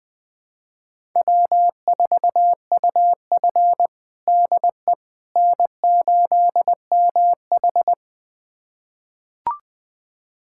Morse code, heard as a keyed tone, about 20 wpm, 700 Hz. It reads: W4UF DE N8MH
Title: USA Callsign DE Callsign - 4 Characters - 20wpm